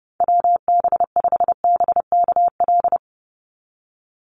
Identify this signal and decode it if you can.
Morse code, heard as a keyed tone, about 30 wpm, 700 Hz. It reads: WB5BKL
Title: NA SOTA Chaser Callsigns 30wpm